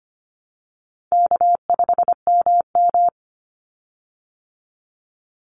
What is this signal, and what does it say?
Morse code, heard as a keyed tone, about 25 wpm, 700 Hz. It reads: K5MM
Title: Easy Call Signs Repeated 3x - 25wpm